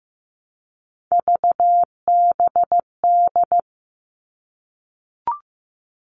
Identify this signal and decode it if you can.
Morse code, heard as a keyed tone, about 15 wpm, 700 Hz. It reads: VBD
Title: All Cut Numbers - 3-Digit 15wpm